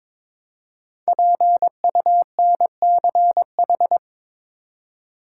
Morse code, heard as a keyed tone, about 22 wpm, 700 Hz. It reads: PUNCH
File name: 5 Letter Words - First 3 Letters Given - 22wpm